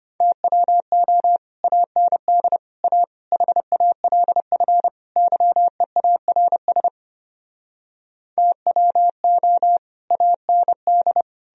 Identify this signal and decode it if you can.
Morse code, heard as a keyed tone, about 33 wpm, 700 Hz. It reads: TWO AND A HALF YEARS TWO AND